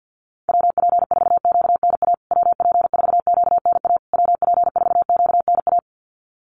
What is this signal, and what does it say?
Morse code, heard as a keyed tone, about 50 wpm, 700 Hz. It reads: WP4QNA WP4QNA WP4QNA
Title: USA Call Signs Repeated 3x 50wpm